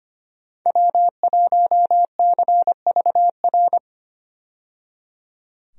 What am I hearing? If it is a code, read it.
Morse code, heard as a keyed tone, about 25 wpm, 700 Hz. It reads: W1CVR
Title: USA Call Signs - 5 Characters - 25wpm